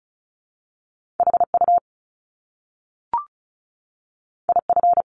Morse code, heard as a keyed tone, about 36 wpm, 700 Hz. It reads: HU IF
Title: Top Two Letter Combinations in Words 35wpm